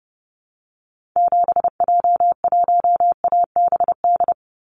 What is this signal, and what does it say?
Morse code, heard as a keyed tone, about 30 wpm, 700 Hz. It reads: 7J1ABD
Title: Call Signs - 30wpm